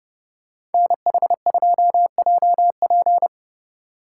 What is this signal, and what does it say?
Morse code, heard as a keyed tone, about 30 wpm, 700 Hz. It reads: NH2JP